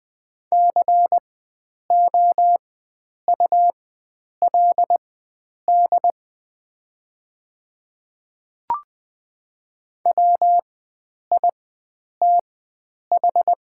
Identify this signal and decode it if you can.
Morse code, heard as a keyed tone, about 20 wpm, spaced as for 9 wpm, 700 Hz. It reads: COULD WITH
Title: 4x Character Spacing - Top 100 Words - 20wpm